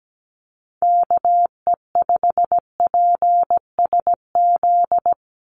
Morse code, heard as a keyed tone, about 17 wpm, 700 Hz. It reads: KE5PSZ
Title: USA Call Signs 17wpm